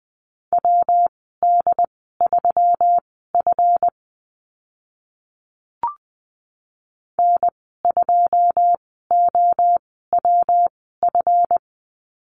Morse code, heard as a keyed tone, about 20 wpm, spaced as for 15 wpm, 700 Hz. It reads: WD3F N2OWF
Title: 140 Lesson 23g - Letter f - Call Signs 20wpm